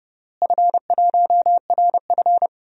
Morse code, heard as a keyed tone, about 31 wpm, 700 Hz. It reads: F1RF
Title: DXCC-100 Call Signs - 4 Characters - Repeated 3x - 30wpm